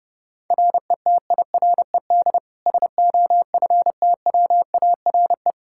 Morse code, heard as a keyed tone, about 30 wpm, 700 Hz. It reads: RETIRED SOFTWARE
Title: Practice QSO - Exchange 3 - More Info - 30wpm